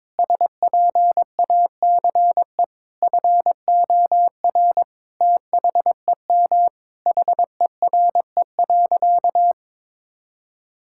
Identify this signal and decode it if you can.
Morse code, heard as a keyed tone, about 22 wpm, 700 Hz. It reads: SPACE FOR THEM HERE.
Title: Sentences from Top 1000 Words - Review Part 5 - 22wpm